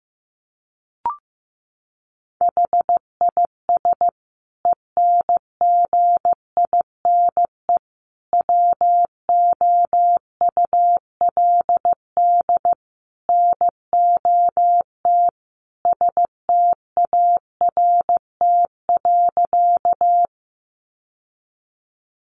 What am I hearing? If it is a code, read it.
Morse code, heard as a keyed tone, about 15 wpm, 700 Hz. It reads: HIS ENGINE WOULD NOT START.
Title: Sentences from Top 1500 Words 15wpm